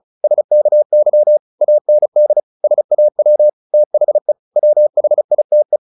Morse code, heard as a keyed tone, about 34 wpm, 600 Hz. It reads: SKY AND SAW THE WHITE
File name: the old man and the sea / the old man and the sea 1 35wpm